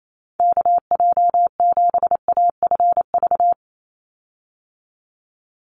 Morse code, heard as a keyed tone, about 28 wpm, 700 Hz. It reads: KJ7AFV
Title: USA Call Signs 28wpm